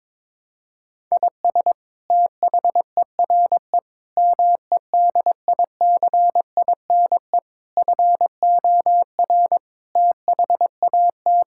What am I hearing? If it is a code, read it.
Morse code, heard as a keyed tone, about 22 wpm, 700 Hz. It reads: IS THERE MEDICINE FOR THAT
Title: Sentences from Top 1000 Words - Review Part 6 - 22wpm